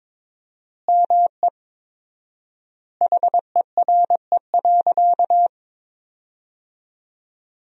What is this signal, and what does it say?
Morse code, heard as a keyed tone, about 22 wpm, 700 Hz. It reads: ME HERE.
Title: 4x Word-Spacing - Sentences from Top 200 Words - 22wpm